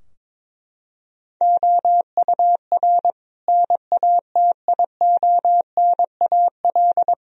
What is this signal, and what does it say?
Morse code, heard as a keyed tone, about 22 wpm, 700 Hz. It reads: OUR NATIONAL
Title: Sentences from Top 1000 Words - Review Part 4 - 22wpm